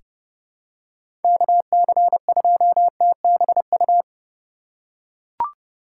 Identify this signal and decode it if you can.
Morse code, heard as a keyed tone, about 32 wpm, 700 Hz. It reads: KC2TBU
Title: USA Call Signs - 6 Characters - 4x Character Spacing - 30wpm